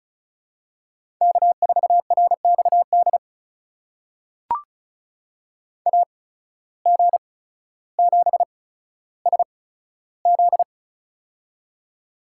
Morse code, heard as a keyed tone, about 34 wpm, spaced as for 15 wpm, 700 Hz. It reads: K4RXD AG7SZ
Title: USA Call Signs - 5 Characters - 8x Character Spacing - 35wpm